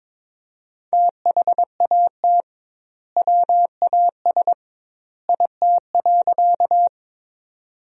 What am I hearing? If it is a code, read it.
Morse code, heard as a keyed tone, about 22 wpm, 700 Hz. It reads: THAT WAS IT.
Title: 2x Word-Spacing - Sentences from Top 100 Words - 22wpm